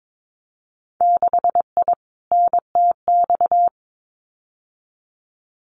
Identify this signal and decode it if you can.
Morse code, heard as a keyed tone, about 22 wpm, 700 Hz. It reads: 6I NTX
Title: Winter Field Day Exchanges Encore 22wpm